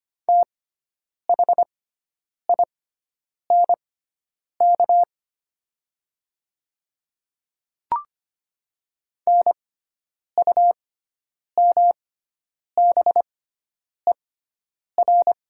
Morse code, heard as a keyed tone, about 25 wpm, spaced as for 9 wpm, 700 Hz. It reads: THINK NUMBER
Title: 6x Character Spacing - Top 300 Words - 25wpm